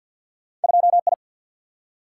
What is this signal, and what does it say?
Morse code, heard as a keyed tone, about 50 wpm, 700 Hz. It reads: 2I
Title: Single-Letter Single-Number 50wpm